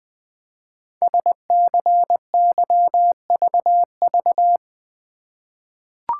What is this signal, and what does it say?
Morse code, heard as a keyed tone, about 20 wpm, 700 Hz. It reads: SCYVV
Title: Code Groups - 5 Letters - 20wpm